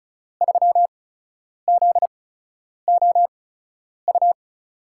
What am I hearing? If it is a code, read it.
Morse code, heard as a keyed tone, about 36 wpm, spaced as for 10 wpm, 700 Hz. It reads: 3ZOU
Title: USA Call Signs - 6 Characters - 8x Character Spacing - 35wpm